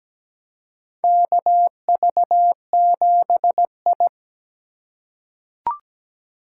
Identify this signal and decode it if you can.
Morse code, heard as a keyed tone, about 17 wpm, 700 Hz. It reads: KV7I
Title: USA Call Signs - 3x Character Spacing - 17wpm